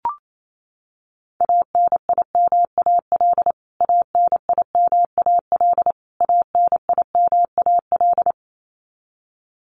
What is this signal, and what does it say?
Morse code, heard as a keyed tone, about 28 wpm, 700 Hz. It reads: ANIMAL ANIMAL ANIMAL
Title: Top 200 Words Repeated 28wpm